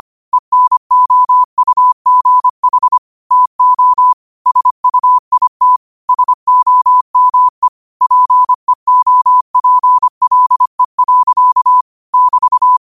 Morse code, heard as a keyed tone, about 25 wpm, 1000 Hz. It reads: ENOUGH TO SUIT SOME PEOPLE. <BT>
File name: qotdfiles / 25wpm